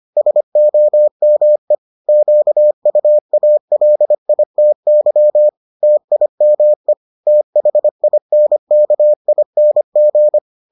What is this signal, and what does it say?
Morse code, heard as a keyed tone, about 25 wpm, 600 Hz. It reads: SOME QUALITY TIME THINKING